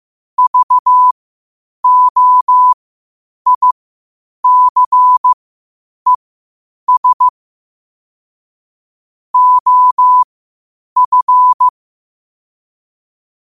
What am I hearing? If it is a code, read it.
Morse code, heard as a keyed tone, about 15 wpm, spaced as for 8 wpm, 1000 Hz. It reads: VOICES OF
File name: qotdfiles / 8wpm